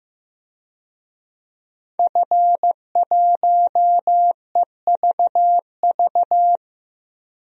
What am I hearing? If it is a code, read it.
Morse code, heard as a keyed tone, about 15 wpm, 700 Hz. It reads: F1EVV
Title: DXCC-100 Call Signs - 5 Characters - Repeated 2x - 15wpm